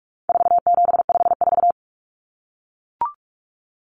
Morse code, heard as a keyed tone, about 44 wpm, 700 Hz. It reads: 4754
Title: Common Cut Numbers - 4-Digit 45wpm